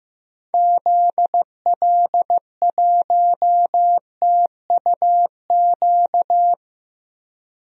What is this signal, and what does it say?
Morse code, heard as a keyed tone, about 15 wpm, 700 Hz. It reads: ZL1TUQ
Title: DXCC-100 Call Signs - 6 Characters - 15wpm